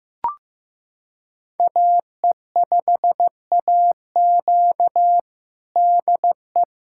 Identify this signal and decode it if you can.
Morse code, heard as a keyed tone, about 15 wpm, 700 Hz. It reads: AE5AQ DE
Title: USA Callsign DE Callsign - 5 Characters - 15wpm